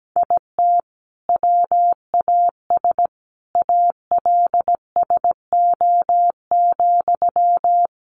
Morse code, heard as a keyed tone, about 17 wpm, 700 Hz. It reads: IT WAS ALSO,
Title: Sentences from Top 1400 Words 17wpm